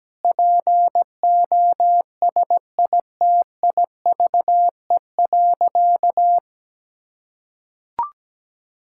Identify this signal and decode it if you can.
Morse code, heard as a keyed tone, about 17 wpm, 700 Hz. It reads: POSITIVE.
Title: Sentences from Top 1000 Words - Review Part 5 - 17wpm